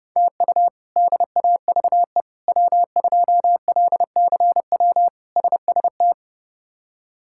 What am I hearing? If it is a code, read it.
Morse code, heard as a keyed tone, about 30 wpm, 700 Hz. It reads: TU DAVE W2LCW SST
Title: K1USN SST QSO Practice - 30wpm